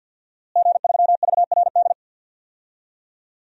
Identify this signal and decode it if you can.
Morse code, heard as a keyed tone, about 52 wpm, 700 Hz. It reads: G3VRD
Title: DXCC-100 Call Signs - 5 Characters - Repeated 2x - 50wpm